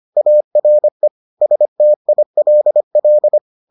Morse code, heard as a keyed tone, about 25 wpm, 600 Hz. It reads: ARE STILL